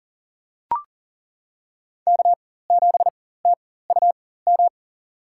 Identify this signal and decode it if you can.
Morse code, heard as a keyed tone, about 40 wpm, spaced as for 19 wpm, 700 Hz. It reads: K7TUM
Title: USA Call Signs - 5 Characters - 4x Character Spacing - 40wpm